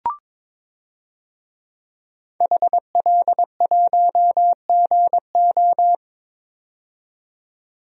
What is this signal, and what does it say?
Morse code, heard as a keyed tone, about 21 wpm, 700 Hz. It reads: HL1GO
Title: DXCC-100 Call Signs - 5 Characters - 22wpm